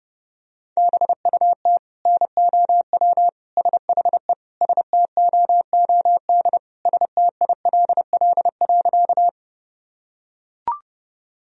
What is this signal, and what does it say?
Morse code, heard as a keyed tone, about 32 wpm, 700 Hz. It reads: BUT NOW SHE STOOD STILL.
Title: Sentences from Top 1000 Words - Review Part 2 - 30wpm